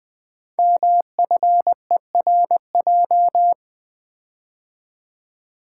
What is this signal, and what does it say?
Morse code, heard as a keyed tone, about 20 wpm, 700 Hz. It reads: MFERJ
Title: Code Groups - 5 Letters - 20wpm